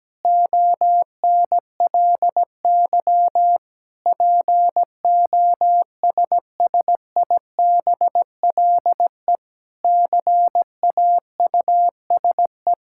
Morse code, heard as a keyed tone, about 17 wpm, 700 Hz. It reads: ONLY POSSIBLE CAUSE
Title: Sentences from Top 1600 Words 17wpm